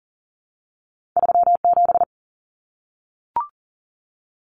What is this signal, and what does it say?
Morse code, heard as a keyed tone, about 40 wpm, 700 Hz. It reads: 37
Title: Numbers - 2-Digit 40wpm